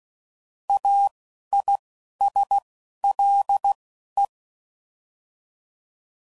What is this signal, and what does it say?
Morse code, heard as a keyed tone, about 16 wpm, spaced as for 11 wpm, 800 Hz. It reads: AISLE
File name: Prefixes / By First Letter / As